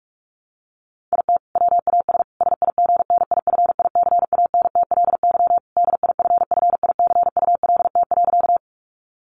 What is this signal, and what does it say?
Morse code, heard as a keyed tone, about 44 wpm, 700 Hz. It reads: IT WAS SIGNIFICANTLY DIFFICULT.
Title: Sentences from Top 1500 Words 45wpm